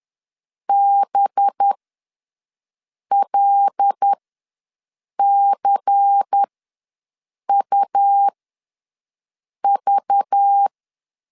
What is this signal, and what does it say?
Morse code, heard as a keyed tone, about 10 wpm, spaced as for 5 wpm, 800 Hz. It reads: BLCUV
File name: LEZIONI / 6 PKX0 / 6 P ws 10wpm 3751